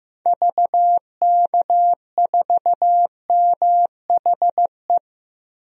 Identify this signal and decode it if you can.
Morse code, heard as a keyed tone, about 15 wpm, 700 Hz. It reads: VK4MHE
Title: DXCC-100 Call Signs - 6 Characters - 15wpm